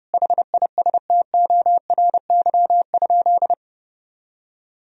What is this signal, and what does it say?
Morse code, heard as a keyed tone, about 30 wpm, 700 Hz. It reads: HISTORY?
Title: Sentences from Top 1000 Words - Review Part 4 - 30wpm